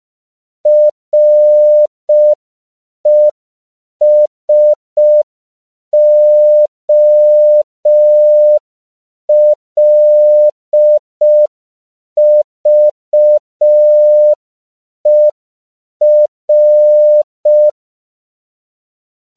Morse code, heard as a keyed tone, about 5 wpm, 600 Hz. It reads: RESOLVER